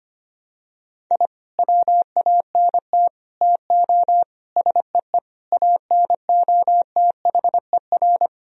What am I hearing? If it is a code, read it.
Morse code, heard as a keyed tone, about 25 wpm, 700 Hz. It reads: I WANT TO SEE ANOTHER
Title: Sentences from Top 1000 Words - Review Part 1 - 25wpm